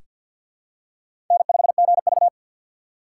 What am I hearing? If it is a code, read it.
Morse code, heard as a keyed tone, about 47 wpm, 700 Hz. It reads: N5GV
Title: USA Call Signs - 4 Characters - 3x Character Spacing - 50wpm